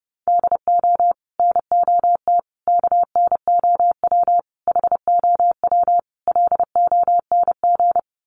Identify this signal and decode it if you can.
Morse code, heard as a keyed tone, about 30 wpm, 700 Hz. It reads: DO NOT KNOW HOW LONG
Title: Sets of 5 Words Encore2 30wpm